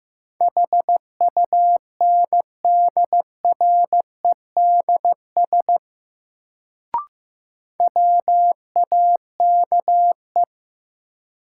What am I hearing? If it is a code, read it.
Morse code, heard as a keyed tone, about 16 wpm, 700 Hz. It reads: HUNDREDS WAKE
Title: Sets of 4 Words - New Word Drill - 15wpm